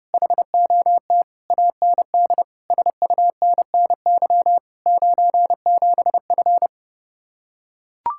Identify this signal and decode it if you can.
Morse code, heard as a keyed tone, about 30 wpm, 700 Hz. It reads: HOT AND SUNNY 97F